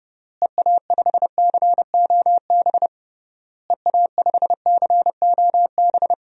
Morse code, heard as a keyed tone, about 30 wpm, 700 Hz. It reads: EA5COB EA5COB
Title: DXCC-100 Call Signs - 6 Characters - Repeated 3x - 30wpm